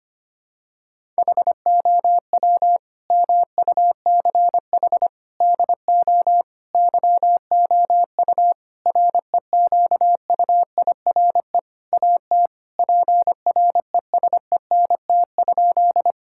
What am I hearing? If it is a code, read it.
Morse code, heard as a keyed tone, about 25 wpm, 700 Hz. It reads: HOW MUCH DO YOU REQUIRE AT PRESENT?
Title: Sentences from Top 1000 Words - Review Part 6 - 25wpm